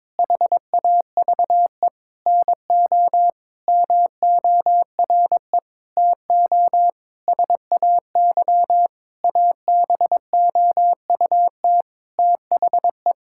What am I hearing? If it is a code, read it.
Morse code, heard as a keyed tone, about 22 wpm, 700 Hz. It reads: HAVE NO MORE TO SAY ABOUT THE
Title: Sentences from Top 1000 Words - Review Part 5 - 22wpm